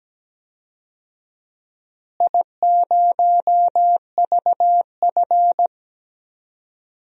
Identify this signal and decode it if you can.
Morse code, heard as a keyed tone, about 17 wpm, 700 Hz. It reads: I0VF
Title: DXCC-100 Call Signs - 4 Characters - Repeated 3x - 17wpm